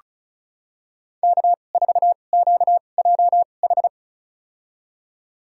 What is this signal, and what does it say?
Morse code, heard as a keyed tone, about 36 wpm, spaced as for 25 wpm, 700 Hz. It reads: K4QJH
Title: USA Call Signs - 5 Characters - 2x Character Spacing - 35wpm